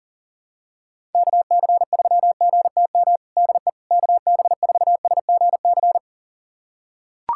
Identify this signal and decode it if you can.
Morse code, heard as a keyed tone, about 37 wpm, 700 Hz. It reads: KC3GTM DE KB4SGC
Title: USA Callsign DE Callsign - 6 Characters - 40wpm